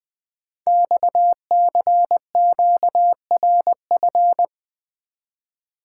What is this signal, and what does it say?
Morse code, heard as a keyed tone, about 20 wpm, 700 Hz. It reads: XCQRF
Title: Code Groups - 5 Letters - 20wpm